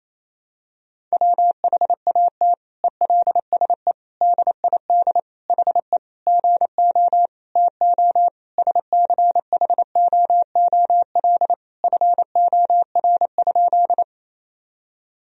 Morse code, heard as a keyed tone, about 28 wpm, 700 Hz. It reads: WHAT ELSE DID HE GO TO SCHOOL FOR?
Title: Sentences from Top 1000 Words - Review Part 1 - 28wpm